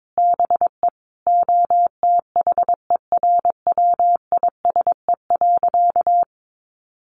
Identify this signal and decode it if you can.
Morse code, heard as a keyed tone, about 22 wpm, 700 Hz. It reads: BE OTHERWISE.
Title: Sentences from Top 1200 Words 22wpm